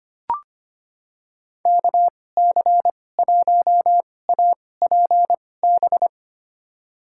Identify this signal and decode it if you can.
Morse code, heard as a keyed tone, about 25 wpm, spaced as for 18 wpm, 700 Hz. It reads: KC1APB
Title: USA Call Signs - 2x Character Spacing - 25wpm